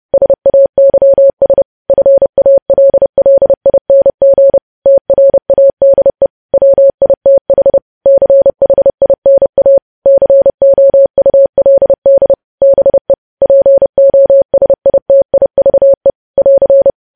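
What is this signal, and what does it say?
Morse code, heard as a keyed tone, about 30 wpm, 550 Hz. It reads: SAYS FALLING TRADE WITH CHINA COULD BE POSITIVE <AR>
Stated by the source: Morse news headlines at 30 wpm.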